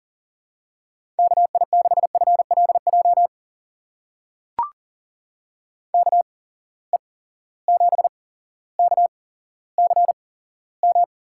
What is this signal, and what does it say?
Morse code, heard as a keyed tone, about 40 wpm, spaced as for 17 wpm, 700 Hz. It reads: KI6FLJ KE7KCM
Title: USA Call Signs - 6 Characters - 8x Character Spacing - 40wpm